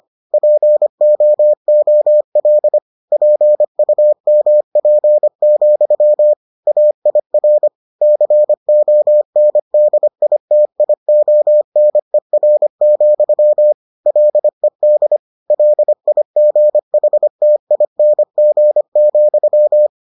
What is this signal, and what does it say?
Morse code, heard as a keyed tone, about 25 wpm, 600 Hz. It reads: POOL PUMP, AIR CONDITIONER, LED LIGHTING,